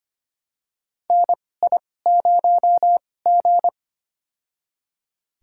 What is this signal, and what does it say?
Morse code, heard as a keyed tone, about 25 wpm, spaced as for 18 wpm, 700 Hz. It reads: NI0G
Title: USA Call Signs - 4 Characters - 2x Character Spacing - 25wpm